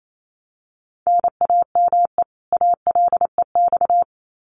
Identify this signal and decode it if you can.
Morse code, heard as a keyed tone, about 28 wpm, 700 Hz. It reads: NAME ALEX
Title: Common QSO Phrases 28wpm